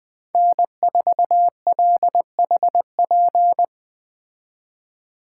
Morse code, heard as a keyed tone, about 19 wpm, 700 Hz. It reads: N4LHP